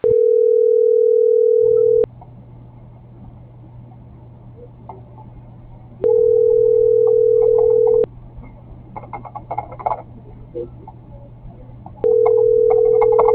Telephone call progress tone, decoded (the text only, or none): ringback tone